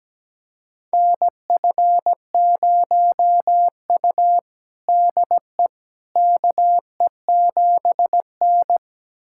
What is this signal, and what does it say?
Morse code, heard as a keyed tone, about 17 wpm, 700 Hz. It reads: NF0U DE KE7N
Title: USA Callsign DE Callsign - 4 Characters - 17wpm